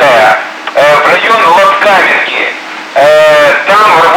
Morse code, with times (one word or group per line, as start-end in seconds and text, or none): none